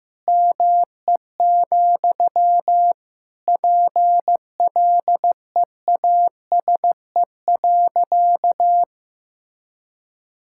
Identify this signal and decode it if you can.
Morse code, heard as a keyed tone, about 15 wpm, 700 Hz. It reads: ME, PLEASE.
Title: Sentences from Top 1600 Words 15wpm